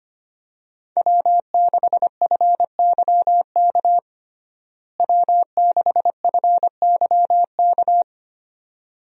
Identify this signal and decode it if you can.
Morse code, heard as a keyed tone, about 25 wpm, 700 Hz. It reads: W6FYK W6FYK
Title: USA Call Signs - 5 Characters - Repeated 2x - 25wpm